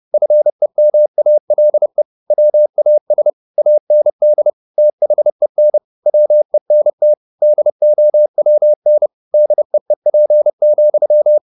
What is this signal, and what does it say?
Morse code, heard as a keyed tone, about 30 wpm, 600 Hz. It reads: FEMALE WAS AND THEN WENT DOWN DEEP,
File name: the old man and the sea / the old man and the sea 1 30wpm